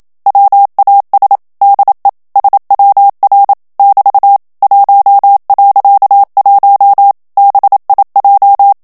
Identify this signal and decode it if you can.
Morse code, heard as a keyed tone, about 27 wpm, 800 Hz. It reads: WAS DE SWR <BT> 1.1 BIJ
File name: HOLLANDAIS texte clair 1 vitesse 26 mots minute